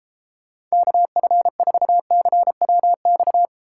Morse code, heard as a keyed tone, about 32 wpm, 700 Hz. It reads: KF4CWX